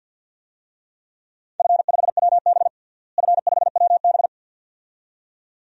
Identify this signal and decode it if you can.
Morse code, heard as a keyed tone, about 52 wpm, 700 Hz. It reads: F5WB F5WB
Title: DXCC-100 Call Signs - 4 Characters - Repeated 2x - 50wpm